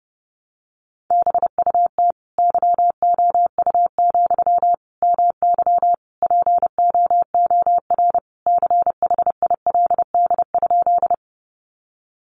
Morse code, heard as a keyed tone, about 28 wpm, 700 Hz. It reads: BUT YOU, MY POOR CHILD?
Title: Sentences from Top 1000 Words 30wpm